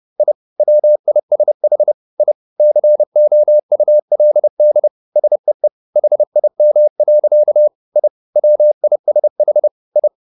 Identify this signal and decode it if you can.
Morse code, heard as a keyed tone, about 30 wpm, 600 Hz. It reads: I WISH I COULD SEE HIM. I WISH I
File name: the old man and the sea / the old man and the sea 1 30wpm